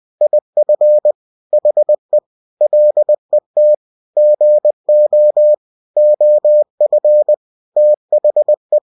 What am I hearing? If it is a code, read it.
Morse code, heard as a keyed tone, about 20 wpm, 600 Hz. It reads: IF HE LET GO OF THE